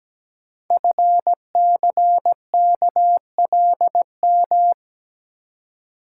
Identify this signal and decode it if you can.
Morse code, heard as a keyed tone, about 17 wpm, 700 Hz. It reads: FCKLM
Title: Code Groups - 5 Letters - 17wpm